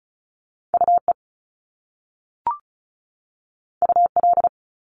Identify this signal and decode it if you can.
Morse code, heard as a keyed tone, about 36 wpm, 700 Hz. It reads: UE UL
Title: Top Two Letter Combinations in Words 35wpm